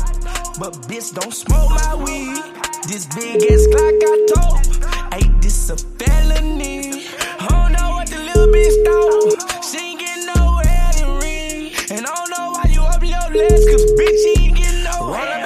Ringback tone (Europe, 425 Hz), with noise.